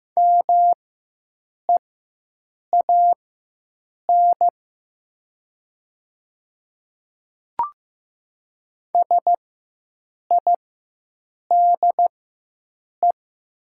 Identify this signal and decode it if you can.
Morse code, heard as a keyed tone, about 15 wpm, spaced as for 7 wpm, 700 Hz. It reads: MEAN SIDE
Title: 4x Character Spacing - Top 200 Words - 15wpm